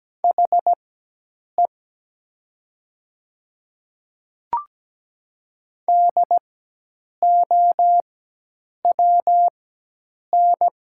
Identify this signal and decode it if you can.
Morse code, heard as a keyed tone, about 17 wpm, spaced as for 8 wpm, 700 Hz. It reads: HE DOWN
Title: 4x Character Spacing - Top 100 Words - 17wpm